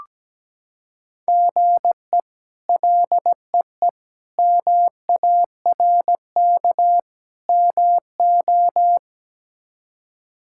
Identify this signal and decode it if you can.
Morse code, heard as a keyed tone, about 17 wpm, 700 Hz. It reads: GE LEE MARK MO
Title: K1USN SST QSO Practice - 17wpm